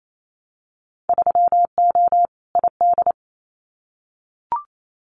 Morse code, heard as a keyed tone, about 28 wpm, 700 Hz. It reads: 3O ID
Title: Winter Field Day Exchanges Encore 28wpm